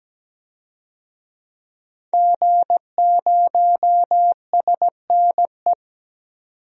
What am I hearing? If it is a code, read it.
Morse code, heard as a keyed tone, about 17 wpm, 700 Hz. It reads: G0SNE
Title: DXCC-100 Call Signs - 5 Characters - 17wpm